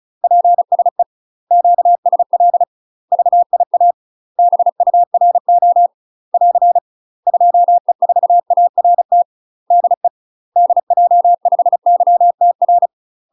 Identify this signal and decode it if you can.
Morse code, heard as a keyed tone, about 34 wpm, 700 Hz. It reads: PSE QSL VIA BURO <AR> 2E4ART DE DJ5YTR
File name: UK MASTER QSOs TRAINING / 35WPM / UK MASTER QSOs 0001